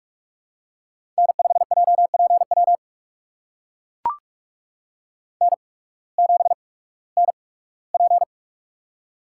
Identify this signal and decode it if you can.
Morse code, heard as a keyed tone, about 44 wpm, spaced as for 21 wpm, 700 Hz. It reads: N5JPW N7NP
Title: USA Call Signs - 8x Character Spacing - 45wpm